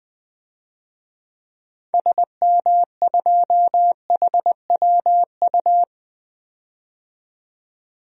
Morse code, heard as a keyed tone, about 21 wpm, 700 Hz. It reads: SM2HWU
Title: DXCC-100 Call Signs - 6 Characters - 20wpm